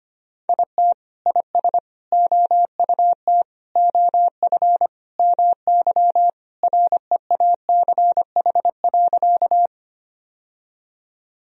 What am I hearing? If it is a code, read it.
Morse code, heard as a keyed tone, about 25 wpm, 700 Hz. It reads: IT IS OUT OF MY REACH.
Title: Sentences from Top 1000 Words - Review Part 2 - 25wpm